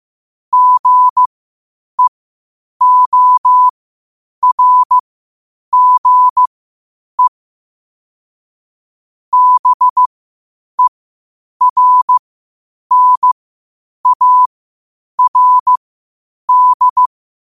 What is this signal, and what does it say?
Morse code, heard as a keyed tone, about 15 wpm, spaced as for 8 wpm, 1000 Hz. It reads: GEORGE BERNARD
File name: qotdfiles / 8wpm